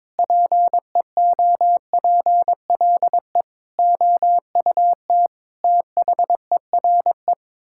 Morse code, heard as a keyed tone, about 22 wpm, 700 Hz. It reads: PEOPLE OUT THERE
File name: practice / 22wpm / Sets of 8 Words - 22wpm